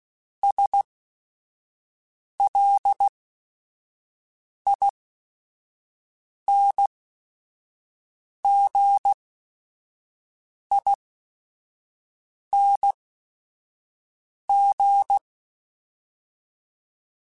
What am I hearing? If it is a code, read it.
Morse code, heard as a keyed tone, about 16 wpm, spaced as for 5 wpm, 800 Hz. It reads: SLINGING